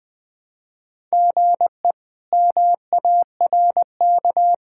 Morse code, heard as a keyed tone, about 20 wpm, 700 Hz. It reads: GE MARK
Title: K1USN SST QSO Practice - 20wpm